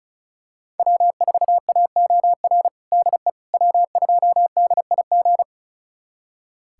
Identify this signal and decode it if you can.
Morse code, heard as a keyed tone, about 36 wpm, 700 Hz. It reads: W4AOR DE W2DIG
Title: USA Callsign DE Callsign - 5 Characters - 35wpm